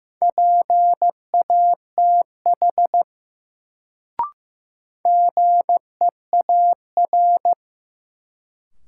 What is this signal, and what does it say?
Morse code, heard as a keyed tone, about 15 wpm, 700 Hz. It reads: PATH GEAR
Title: Sentences from Top 1400 Words - New Word Drill - 15wpm